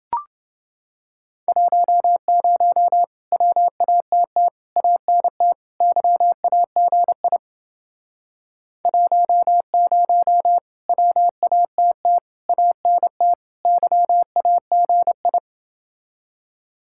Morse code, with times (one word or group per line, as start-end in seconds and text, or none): 1.5-3.0 10
3.3-4.5 WATT
4.8-5.5 ANT
5.8-7.4 YAGI
8.9-10.6 10
10.9-12.2 WATT
12.5-13.3 ANT
13.7-15.4 YAGI